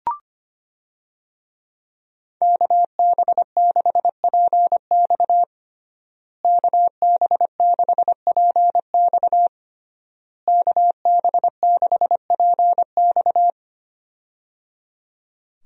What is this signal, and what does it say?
Morse code, heard as a keyed tone, about 25 wpm, 700 Hz. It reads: KB6PX KB6PX KB6PX
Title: USA Call Signs - 5 Characters - Repeated 3x - 25wpm